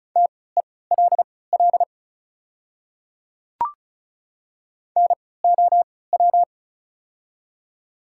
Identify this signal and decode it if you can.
Morse code, heard as a keyed tone, about 35 wpm, spaced as for 20 wpm, 700 Hz. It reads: TELL NOW
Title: 3x Character Spacing - Top 200 Words - 35wpm